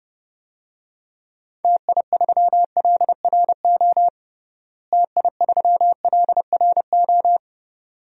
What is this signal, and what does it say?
Morse code, heard as a keyed tone, about 32 wpm, 700 Hz. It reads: TI3LRO TI3LRO
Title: DXCC-100 Call Signs - 6 Characters - Repeated 2x - 30wpm